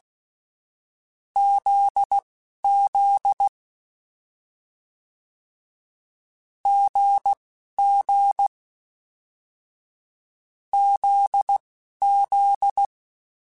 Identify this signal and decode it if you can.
Morse code, heard as a keyed tone, about 16 wpm, spaced as for 11 wpm, 800 Hz. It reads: ZZ GG ZZ
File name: Short Runs / Short Runs-mp3s / GZ GZ7 @10